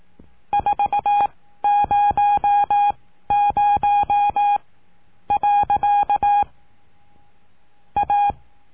Morse code, heard as a keyed tone, about 18 wpm, spaced as for 11 wpm, 820 Hz. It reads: 400. A
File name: cursos / CURSO DE TELEGRAFIA / Telegrafia - Trilha 19